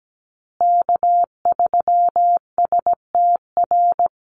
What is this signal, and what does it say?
Morse code, heard as a keyed tone, about 17 wpm, 700 Hz. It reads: K3STR
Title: USA Call Signs Repeated 2x 17wpm